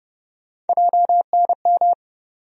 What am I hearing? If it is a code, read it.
Morse code, heard as a keyed tone, about 31 wpm, 700 Hz. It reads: JNM
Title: USA Call Sign Postfixes - 3 Letters 30wpm